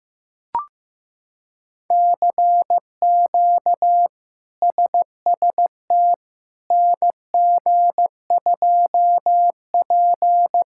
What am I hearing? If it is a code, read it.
Morse code, heard as a keyed tone, about 15 wpm, 700 Hz. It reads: CQ SST NG2P
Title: K1USN SST QSO Practice - 15wpm